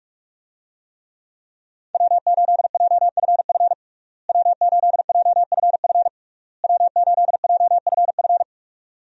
Morse code, heard as a keyed tone, about 46 wpm, 700 Hz. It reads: W8JFF W8JFF W8JFF
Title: USA Call Signs - 5 Characters - Repeated 3x - 45wpm